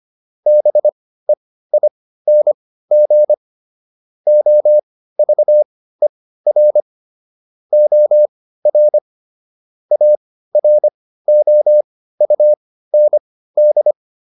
Morse code, heard as a keyed tone, about 25 wpm, spaced as for 15 wpm, 600 Hz. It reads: BEING OVER OR AROUND